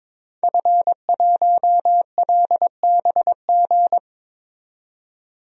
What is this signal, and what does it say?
Morse code, heard as a keyed tone, about 22 wpm, 700 Hz. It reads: F1LBG
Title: DXCC-100 Call Signs - 5 Characters - 22wpm